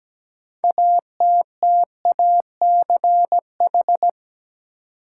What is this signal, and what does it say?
Morse code, heard as a keyed tone, about 17 wpm, 700 Hz. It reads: ATTACH
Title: Six Letter Words - 17wpm